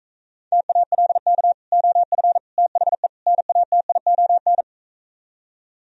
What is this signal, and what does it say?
Morse code, heard as a keyed tone, about 45 wpm, 690 Hz. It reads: TALK OF THE NATION